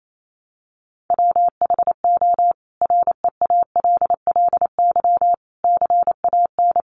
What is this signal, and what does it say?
Morse code, heard as a keyed tone, about 28 wpm, 700 Hz. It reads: WHO REALLY CAN
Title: A Sound Of Thunder 28wpm